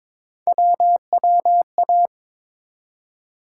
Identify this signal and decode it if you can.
Morse code, heard as a keyed tone, about 22 wpm, 700 Hz. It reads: WWA
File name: ARRL Sections v2 - 22wpm